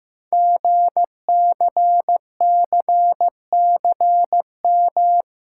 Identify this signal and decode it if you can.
Morse code, heard as a keyed tone, about 15 wpm, 700 Hz. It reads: GCCCM
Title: Code Groups - 5 Letters - 15wpm